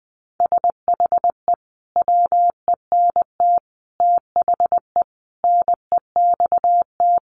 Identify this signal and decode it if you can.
Morse code, heard as a keyed tone, about 20 wpm, 700 Hz. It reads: SHE WENT THE NEXT
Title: Sentences from Top 1500 Words 20wpm